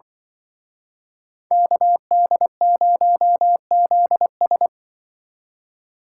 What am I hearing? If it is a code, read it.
Morse code, heard as a keyed tone, about 25 wpm, 700 Hz. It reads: KD0ZS
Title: Speed Racing - USA 2x2 Call Signs 20wpm